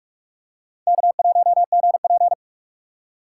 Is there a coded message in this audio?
Yes: Morse code from 0.9 to 2.3 s, heard as a keyed tone, about 44 wpm, 700 Hz. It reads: K1GP